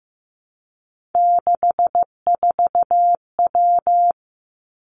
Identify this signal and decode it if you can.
Morse code, heard as a keyed tone, about 15 wpm, 700 Hz. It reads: 64W
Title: Three Challenging Letter-Number Combinations 15wpm